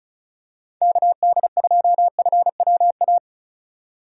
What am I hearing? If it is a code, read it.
Morse code, heard as a keyed tone, about 36 wpm, 700 Hz. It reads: KD2FWA
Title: USA Call Signs - 6 Characters - 35wpm